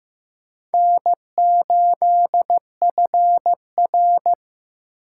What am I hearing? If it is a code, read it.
Morse code, heard as a keyed tone, about 15 wpm, 700 Hz. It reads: N8FR
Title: USA Call Signs - 4 Characters - 3x Character Spacing - 15wpm